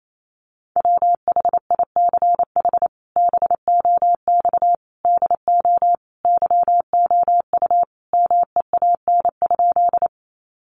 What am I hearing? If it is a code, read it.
Morse code, heard as a keyed tone, about 28 wpm, 700 Hz. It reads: WHICH BOX DO YOU MEAN?
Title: Sentences from Top 500 Words 28wpm